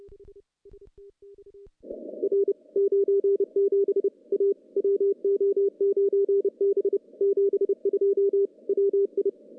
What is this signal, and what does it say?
Morse code, heard as a keyed tone, about 30 wpm, 400 Hz. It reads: R 97 A WO9B 72 WI
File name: contests / CW-SS-k5tr- -0400